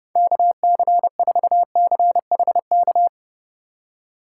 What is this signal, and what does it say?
Morse code, heard as a keyed tone, about 30 wpm, 700 Hz. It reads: KC4CHK
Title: USA Call Signs - 3x Character Spacing - 30wpm